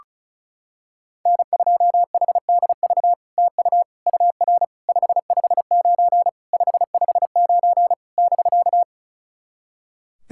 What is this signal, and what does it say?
Morse code, heard as a keyed tone, about 36 wpm, 700 Hz. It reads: N2HDV TU UR 559 559 <BK>
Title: SOTA QSO Practice 35wpm